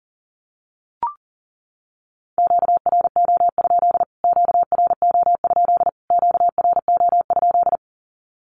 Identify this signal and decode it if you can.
Morse code, heard as a keyed tone, about 40 wpm, 700 Hz. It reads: QRO? QRO? QRO?
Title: Q-Codes Repeated 3x 40wpm